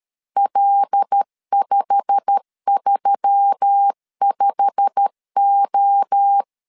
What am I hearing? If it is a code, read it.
Morse code, heard as a keyed tone, about 13 wpm, 800 Hz. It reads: L535O
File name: LEZIONI / 8 YJ45 / 8 YJ45 12wpm 2151